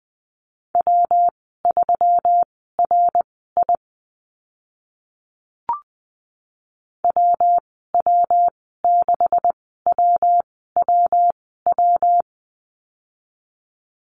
Morse code, heard as a keyed tone, about 20 wpm, spaced as for 14 wpm, 700 Hz. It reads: W3RI WW6WWW
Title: 140 Lesson 23g - Letter f - Call Signs 20wpm